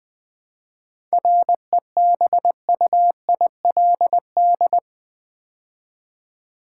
Morse code, heard as a keyed tone, about 20 wpm, 700 Hz. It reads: REBUILD
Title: 7 Letter Words - First 3 Letters Given - 20wpm